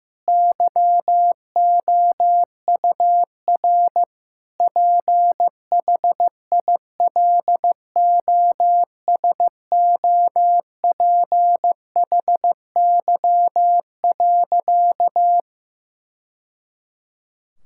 Morse code, heard as a keyed tone, about 15 wpm, 700 Hz. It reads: YOUR PHILOSOPHY.
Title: Sentences from Top 1000 Words - Review Part 6 - 15wpm